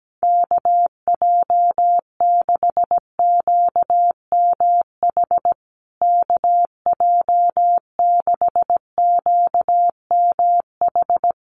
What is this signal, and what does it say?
Morse code, heard as a keyed tone, about 17 wpm, 700 Hz. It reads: KJ6QMH KJ6QMH
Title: USA Call Signs Repeated 2x 17wpm